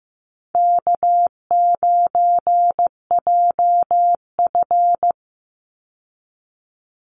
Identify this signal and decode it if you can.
Morse code, heard as a keyed tone, about 15 wpm, 700 Hz. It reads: K9JF
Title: Easy Call Signs Repeated 2x - 15wpm